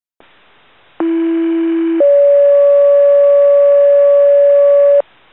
Two-tone page, heard as a one-second tone, then a three-second tone, about 330 then 570 Hz. It is heard from 1.0 to 5.0 s.